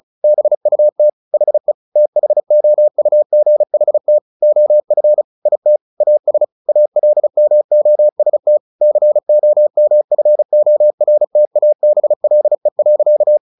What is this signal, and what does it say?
Morse code, heard as a keyed tone, about 34 wpm, 600 Hz. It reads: BUT HE THOUGHT OF IT AS ALMOST COMFORTABLE.
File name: the old man and the sea / the old man and the sea 1 35wpm